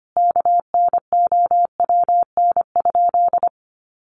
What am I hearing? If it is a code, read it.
Morse code, heard as a keyed tone, about 25 wpm, 700 Hz. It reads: KNOWN?
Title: Sentences from Top 700 Words 25wpm